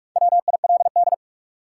Morse code, heard as a keyed tone, about 45 wpm, 700 Hz. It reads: WILD
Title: Four Letter Words - 45wpm